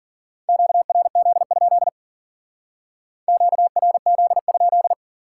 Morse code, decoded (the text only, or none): QRZ? QRZ?